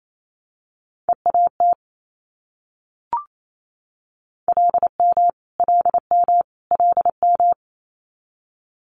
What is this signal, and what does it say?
Morse code, heard as a keyed tone, about 28 wpm, 700 Hz. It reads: EAT LM LM LM
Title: Multi-letter Phonemes Repeated 3x 28wpm